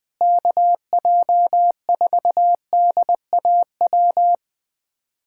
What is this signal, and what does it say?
Morse code, heard as a keyed tone, about 21 wpm, 700 Hz. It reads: KJ4DAW
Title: USA Callsign DE Callsign - 6 Characters - 20wpm